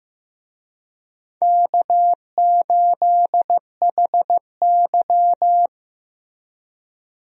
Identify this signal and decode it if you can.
Morse code, heard as a keyed tone, about 15 wpm, 700 Hz. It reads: K8HY
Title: USA Call Signs - 4 Characters - Repeated 2x - 15wpm